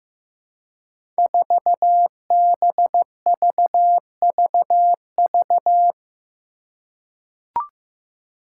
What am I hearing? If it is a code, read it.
Morse code, heard as a keyed tone, about 15 wpm, 700 Hz. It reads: 4BVVV
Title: Five Challenging Letter-Number Combinations 15wpm